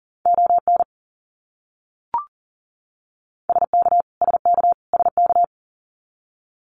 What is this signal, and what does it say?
Morse code, heard as a keyed tone, about 40 wpm, 700 Hz. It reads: ON SK SK SK
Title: Canadian Provinces and Territories - Repeated 3x - 40wpm